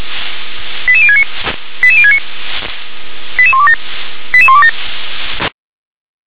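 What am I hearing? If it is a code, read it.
Five-tone selective call (ZVEI), heard as quick runs of five tones, tones of about 70 ms. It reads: 88069 88069 88127 88127
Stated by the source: ZVEI five-tone selective call received on an SDR.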